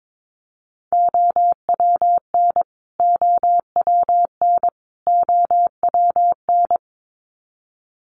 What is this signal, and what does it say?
Morse code, heard as a keyed tone, about 22 wpm, 700 Hz. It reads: OWN OWN OWN
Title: Top 300 Words Repeated 22wpm